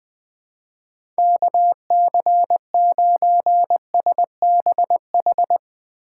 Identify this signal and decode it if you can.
Morse code, heard as a keyed tone, about 20 wpm, 700 Hz. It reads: KC9SBH
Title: USA Call Signs - 2x Character Spacing - 20wpm